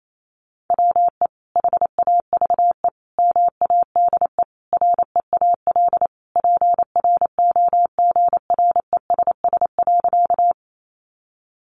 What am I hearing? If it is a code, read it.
Morse code, heard as a keyed tone, about 28 wpm, 700 Hz. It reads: WE HAVE MADE REAL PROGRESS.
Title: Sentences from Top 1400 Words 28wpm